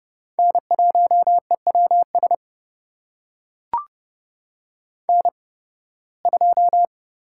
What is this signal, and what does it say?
Morse code, heard as a keyed tone, about 29 wpm, spaced as for 19 wpm, 700 Hz. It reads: N1EWS N2
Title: USA Call Signs - 5 Characters - 8x Character Spacing - 30wpm